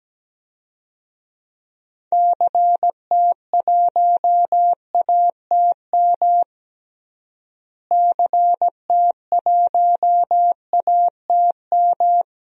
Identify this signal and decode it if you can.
Morse code, heard as a keyed tone, about 17 wpm, 700 Hz. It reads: CT1ATM CT1ATM
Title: DXCC-100 Call Signs - 6 Characters - Repeated 2x - 17wpm